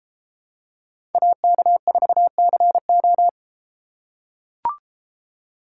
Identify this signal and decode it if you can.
Morse code, heard as a keyed tone, about 31 wpm, 700 Hz. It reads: AK4CO